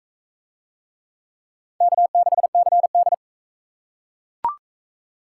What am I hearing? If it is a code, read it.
Morse code, heard as a keyed tone, about 42 wpm, 700 Hz. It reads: K6CD